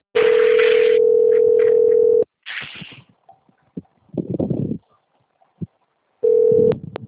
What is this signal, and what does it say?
Ringback tone (North America), 440 + 480 Hz.
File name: out- - - - - .42352